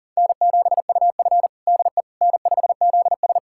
Morse code, heard as a keyed tone, about 40 wpm, 700 Hz. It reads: N7UF DE N5ZS
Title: USA Callsign DE Callsign - 4 Characters - 40wpm